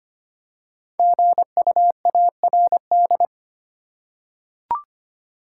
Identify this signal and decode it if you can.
Morse code, heard as a keyed tone, about 25 wpm, 700 Hz. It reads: GUARD
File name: 5 Letter Words - First 3 Letters Given - 25wpm